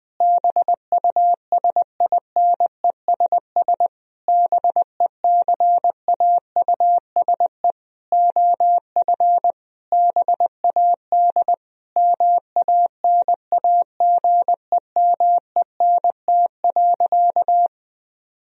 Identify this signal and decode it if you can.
Morse code, heard as a keyed tone, about 19 wpm, 700 Hz. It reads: BUSINESS BECAUSE OF BAD MANAGEMENT.
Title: Sentences from Top 1000 Words - Review Part 4 - 20wpm